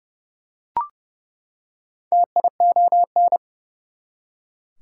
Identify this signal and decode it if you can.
Morse code, heard as a keyed tone, about 31 wpm, 700 Hz. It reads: TION
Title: Prefixes and Suffixes 30wpm